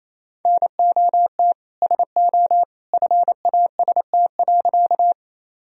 Morse code, heard as a keyed tone, about 28 wpm, 700 Hz. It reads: NOT SO FAST.
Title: Sentences from Top 1000 Words - Review Part 2 - 28wpm